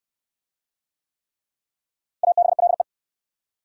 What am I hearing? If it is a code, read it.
Morse code, heard as a keyed tone, about 60 wpm, 700 Hz. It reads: ABLE